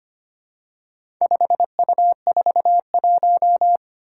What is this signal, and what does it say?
Morse code, heard as a keyed tone, about 25 wpm, 700 Hz. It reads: 5U41
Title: Four Challenging Letter-Number Combinations 25wpm